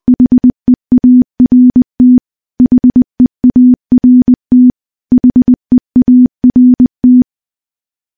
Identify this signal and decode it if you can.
Morse code, heard as a keyed tone, about 20 wpm, 250 Hz. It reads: HEART HEART HEART
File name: morse / morse-training-20WPM